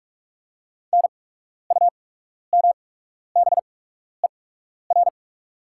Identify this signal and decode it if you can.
Morse code, heard as a keyed tone, about 44 wpm, spaced as for 12 wpm, 700 Hz. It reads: NUMBER
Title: 8x Character Spacing - Top 200 Words - 45wpm